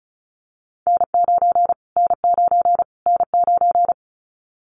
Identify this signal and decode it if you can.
Morse code, heard as a keyed tone, about 34 wpm, 700 Hz. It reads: N9 N9 N9
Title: Callsign Prefixes for Top 25 DXCC - Repeated 3x - 35wpm